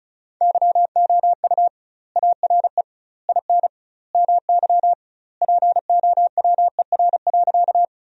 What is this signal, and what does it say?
Morse code, heard as a keyed tone, about 34 wpm, 700 Hz. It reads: YOU ARE IN MY POWER.
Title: 2x Word-Spacing - Sentences from Top 500 Words - 35wpm